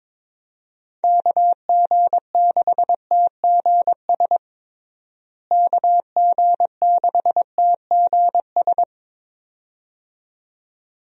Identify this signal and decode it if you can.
Morse code, heard as a keyed tone, about 22 wpm, 700 Hz. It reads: KG6TGS KG6TGS
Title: USA Call Signs - 6 Characters - Repeated 2x - 22wpm